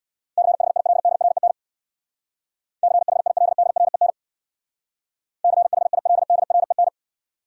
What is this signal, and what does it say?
Morse code, heard as a keyed tone, about 68 wpm, 700 Hz. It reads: CHILDREN CHILDREN CHILDREN